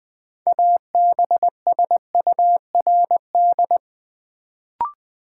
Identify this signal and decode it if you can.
Morse code, heard as a keyed tone, about 20 wpm, 700 Hz. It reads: ABSURD
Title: Build A Word - 3 Letters At A Time - 6-Letter Words - 20wpm